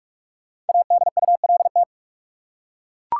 Morse code, heard as a keyed tone, about 42 wpm, 700 Hz. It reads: ADULT
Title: Five Letter Words - 45wpm